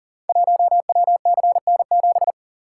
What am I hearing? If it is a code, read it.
Morse code, heard as a keyed tone, about 38 wpm, 690 Hz. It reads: 1WCN7
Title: Code Groups - 5 Letter:Numbers - 40wpm